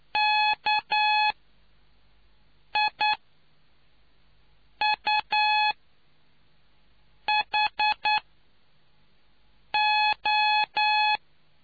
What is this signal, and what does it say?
Morse code, heard as a keyed tone, about 9 wpm, spaced as for 4 wpm, 850 Hz. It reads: KIUHO